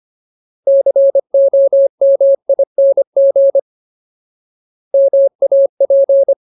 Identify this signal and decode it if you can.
Morse code, heard as a keyed tone, about 25 wpm, 550 Hz. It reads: COMING MAP
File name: Ultimate Word List / 25WPM Ultimate